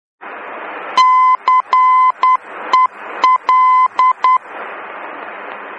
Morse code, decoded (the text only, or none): CEL